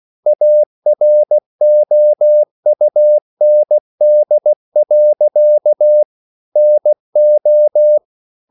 Morse code, heard as a keyed tone, about 16 wpm, 600 Hz. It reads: AROUND. NO